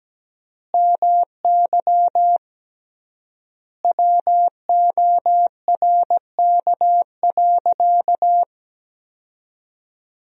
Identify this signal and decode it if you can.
Morse code, heard as a keyed tone, about 17 wpm, 700 Hz. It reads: MY WORK.
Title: 3x Word-Spacing - Sentences from Top 200 Words - 17wpm